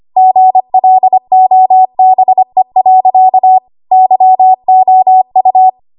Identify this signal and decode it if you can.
Morse code, heard as a keyed tone, about 25 wpm, 750 Hz. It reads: GLOBE. YOU
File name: Archive / 25 WPM / 25WPM